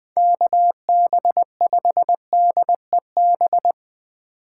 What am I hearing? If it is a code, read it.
Morse code, heard as a keyed tone, about 21 wpm, 700 Hz. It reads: KB5DEB
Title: Speed Racing - USA Call Signs 17wpm